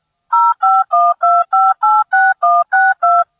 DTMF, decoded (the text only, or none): 0512586162